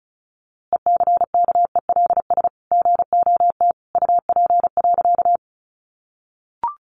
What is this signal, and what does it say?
Morse code, heard as a keyed tone, about 34 wpm, 700 Hz. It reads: ECKELS GOT UP.
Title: A Sound Of Thunder 35wpm